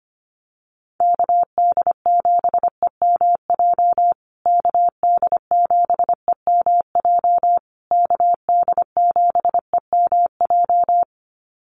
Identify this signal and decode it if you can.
Morse code, heard as a keyed tone, about 25 wpm, 700 Hz. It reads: KD7EMJ KD7EMJ KD7EMJ
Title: USA Call Signs Repeated 3x 25wpm